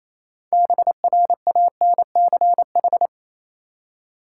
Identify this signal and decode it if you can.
Morse code, heard as a keyed tone, about 28 wpm, 700 Hz. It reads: BRANCH
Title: Six Letter Words - 28wpm